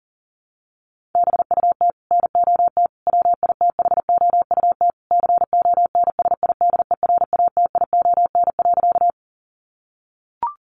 Morse code, heard as a keyed tone, about 40 wpm, 700 Hz. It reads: BUT NOT WITHOUT CONSIDERATION.
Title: Sentences from Top 1500 Words 40wpm